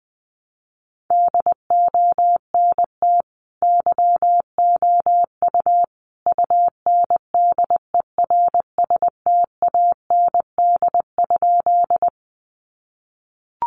Morse code, heard as a keyed tone, about 20 wpm, 700 Hz. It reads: DONT YOU UNDERSTAND?
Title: Sentences from Top 600 Words 20wpm